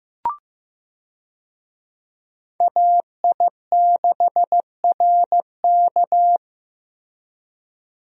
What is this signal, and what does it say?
Morse code, heard as a keyed tone, about 15 wpm, 700 Hz. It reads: AI6RK
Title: USA Call Signs - 5 Characters - Repeated 3x - 15wpm